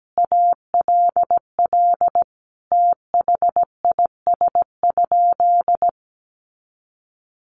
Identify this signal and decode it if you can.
Morse code, heard as a keyed tone, about 17 wpm, 700 Hz. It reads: ALL THIS?
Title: Sentences from Top 100 Words 17wpm